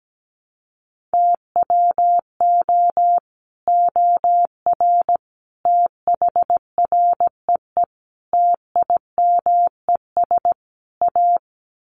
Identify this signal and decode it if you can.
Morse code, heard as a keyed tone, about 17 wpm, 700 Hz. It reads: TWO OR THREE TIMES A
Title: Sets of 5 Words Encore2 17wpm